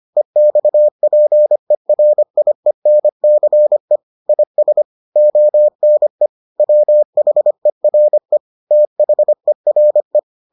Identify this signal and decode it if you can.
Morse code, heard as a keyed tone, about 25 wpm, 600 Hz. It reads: EXPERIENCE IS ONE WHERE THERE